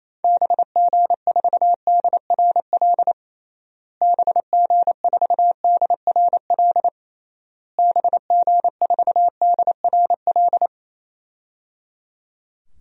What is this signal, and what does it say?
Morse code, heard as a keyed tone, about 28 wpm, 700 Hz. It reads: BG4DRL BG4DRL BG4DRL
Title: DXCC-100 Call Signs - 6 Characters - Repeated 3x - 28wpm